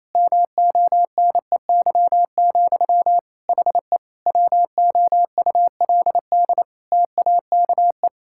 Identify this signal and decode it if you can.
Morse code, heard as a keyed tone, about 28 wpm, 700 Hz. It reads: MONEY, HE WOULD TAKE
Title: Sentences from Top 1000 Words - Review Part 6 - 28wpm